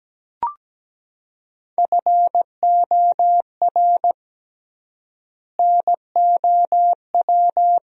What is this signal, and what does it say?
Morse code, heard as a keyed tone, about 17 wpm, 700 Hz. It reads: FOR NOW
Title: 3x Word-Spacing - Sentences from Top 200 Words - 17wpm